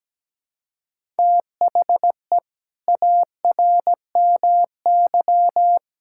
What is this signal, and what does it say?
Morse code, heard as a keyed tone, about 17 wpm, 700 Hz. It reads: THE ARMY
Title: Sentences from Top 1000 Words - Review Part 5 - 17wpm